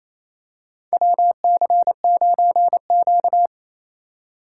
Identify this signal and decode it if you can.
Morse code, heard as a keyed tone, about 28 wpm, 700 Hz. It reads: WC9Q
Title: USA Call Signs - 4 Characters - 3x Character Spacing - 28wpm